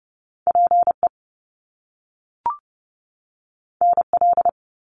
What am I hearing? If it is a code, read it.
Morse code, heard as a keyed tone, about 30 wpm, 700 Hz. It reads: PE NL
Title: Canadian Provinces and Territories 30wpm